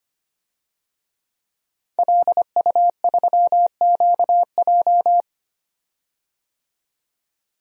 Morse code, heard as a keyed tone, about 25 wpm, 700 Hz. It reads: LU3QJ
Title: DXCC-100 Call Signs - 5 Characters - 25wpm